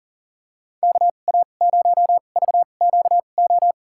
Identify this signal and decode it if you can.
Morse code, heard as a keyed tone, about 40 wpm, spaced as for 29 wpm, 700 Hz. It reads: KA0VQO
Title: USA Call Signs - 6 Characters - 2x Character Spacing - 40wpm